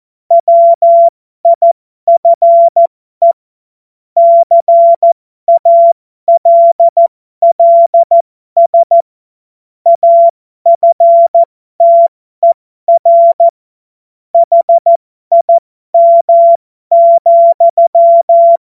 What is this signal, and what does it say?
Morse code, heard as a keyed tone, about 14 wpm, spaced as for 12 wpm, 680 Hz. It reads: WIFE CALLS AFTER HIM,
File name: Morsekurs / Witze